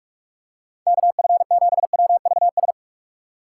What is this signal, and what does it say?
Morse code, heard as a keyed tone, about 45 wpm, 700 Hz. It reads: KF7WVS